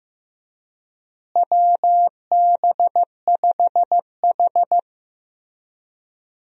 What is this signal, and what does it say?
Morse code, heard as a keyed tone, about 15 wpm, 700 Hz. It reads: WB5H
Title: USA Call Signs - 4 Characters - Repeated 3x - 15wpm